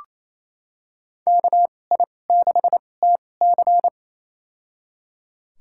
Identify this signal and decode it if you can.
Morse code, heard as a keyed tone, about 28 wpm, spaced as for 20 wpm, 700 Hz. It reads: KI6TC
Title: USA Call Signs - 5 Characters - 2x Character Spacing - 28wpm